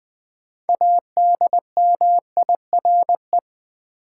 Morse code, heard as a keyed tone, about 20 wpm, 700 Hz. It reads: ADMIRE
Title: 6 Letter Words - First 3 Letters Given - 20wpm